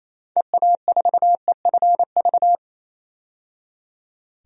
Morse code, heard as a keyed tone, about 28 wpm, 700 Hz. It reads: EA4EFV
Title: DXCC-100 Call Signs - 6 Characters - 28wpm